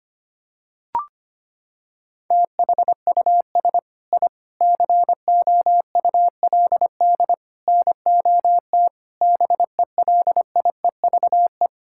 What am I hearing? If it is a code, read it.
Morse code, heard as a keyed tone, about 25 wpm, 700 Hz. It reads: THUS I COULD NOT BELIEVE
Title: Sentences from Top 1000 Words - Review Part 5 - 25wpm